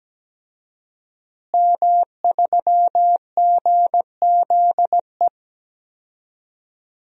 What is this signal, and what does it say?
Morse code, heard as a keyed tone, about 17 wpm, 700 Hz. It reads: M3GZE
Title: DXCC-100 Call Signs - 5 Characters - Repeated 2x - 17wpm